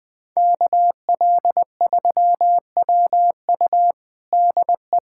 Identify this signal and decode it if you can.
Morse code, heard as a keyed tone, about 21 wpm, 700 Hz. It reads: KL3WU DE
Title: USA Callsign DE Callsign 20wpm